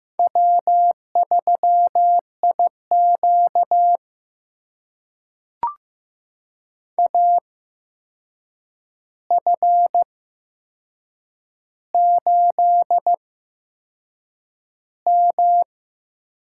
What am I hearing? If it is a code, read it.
Morse code, heard as a keyed tone, about 15 wpm, spaced as for 6 wpm, 700 Hz. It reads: W3IQ AF8M
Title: USA Call Signs - 8x Character Spacing - 15wpm